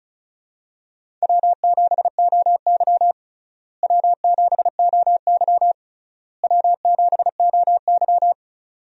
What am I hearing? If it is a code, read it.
Morse code, heard as a keyed tone, about 36 wpm, 700 Hz. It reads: W7OY W7OY W7OY
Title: USA Call Signs - 4 Characters - Repeated 3x - 35wpm